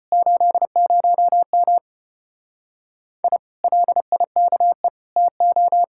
Morse code, heard as a keyed tone, about 33 wpm, 700 Hz. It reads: 80M I LIKE TO